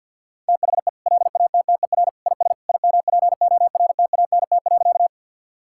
Morse code, heard as a keyed tone, about 50 wpm, 700 Hz. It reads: THE LATTER IS IMPORTANT.
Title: Sentences from Top 1600 Words 50wpm